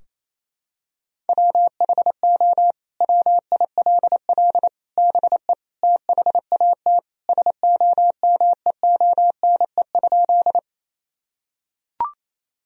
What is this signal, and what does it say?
Morse code, heard as a keyed tone, about 28 wpm, 700 Hz. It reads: WHO WILL BE THAT SOMEONE?
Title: Sentences from Top 1000 Words - Review Part 4 - 28wpm